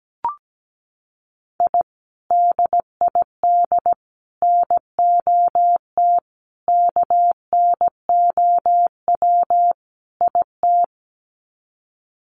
Morse code, heard as a keyed tone, about 17 wpm, 700 Hz. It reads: I DID NOT KNOW IT
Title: Sets of 5 Words Encore2 17wpm